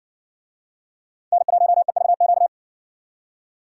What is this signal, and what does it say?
Morse code, heard as a keyed tone, about 57 wpm, 700 Hz. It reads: N1EVQ